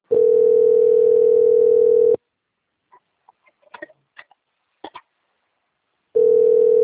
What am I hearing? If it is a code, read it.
Ringback tone (North America), 440 + 480 Hz.
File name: out- - - - - .4460